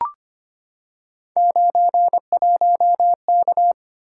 Morse code, heard as a keyed tone, about 25 wpm, 700 Hz. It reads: 91K
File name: Three Letter-Number Combinations - 25wpm